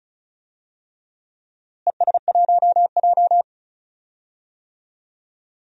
Morse code, heard as a keyed tone, about 36 wpm, 700 Hz. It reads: ES1J
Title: DXCC-100 Call Signs - 4 Characters - Repeated 3x - 35wpm